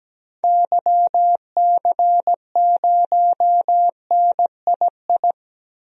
Morse code, heard as a keyed tone, about 17 wpm, 700 Hz. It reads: YC0NII
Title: DXCC-100 Call Signs - 6 Characters - Repeated 2x - 17wpm